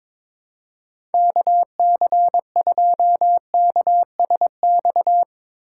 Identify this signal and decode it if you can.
Morse code, heard as a keyed tone, about 22 wpm, 700 Hz. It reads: KC2KSX
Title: USA Call Signs - 6 Characters - 2x Character Spacing - 22wpm